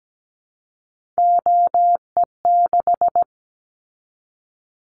Morse code, heard as a keyed tone, about 17 wpm, 700 Hz. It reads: OE6
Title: Callsign Prefixes for Top 50 DXCC - 17wpm